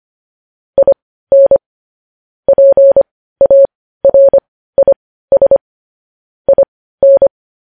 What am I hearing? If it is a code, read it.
Morse code, heard as a keyed tone, about 25 wpm, spaced as for 15 wpm, 550 Hz. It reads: IN PARIS IN
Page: news-headlines-in-morse-code-at-15-wpm